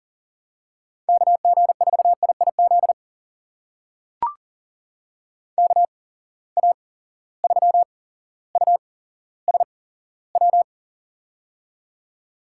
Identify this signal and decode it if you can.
Morse code, heard as a keyed tone, about 40 wpm, spaced as for 17 wpm, 700 Hz. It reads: KG4IIZ KA3USW
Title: USA Call Signs - 6 Characters - 8x Character Spacing - 40wpm